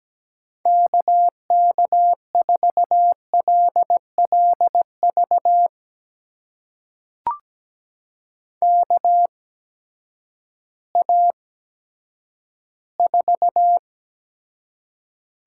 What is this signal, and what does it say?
Morse code, heard as a keyed tone, about 17 wpm, spaced as for 10 wpm, 700 Hz. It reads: KK4LLV KA4
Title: USA Call Signs - 6 Characters - 8x Character Spacing - 17wpm